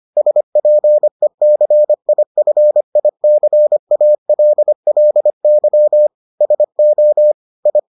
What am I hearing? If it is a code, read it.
Morse code, heard as a keyed tone, about 25 wpm, 600 Hz. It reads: SPECIFICALLY SO I